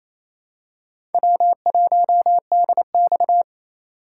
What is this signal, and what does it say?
Morse code, heard as a keyed tone, about 27 wpm, 700 Hz. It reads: W1DX
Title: K1USN SST QSO Practice - 28wpm